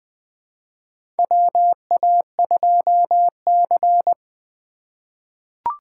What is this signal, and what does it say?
Morse code, heard as a keyed tone, about 20 wpm, 700 Hz. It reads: WA2C
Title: USA Call Signs - 4 Characters - 8x Character Spacing - 20wpm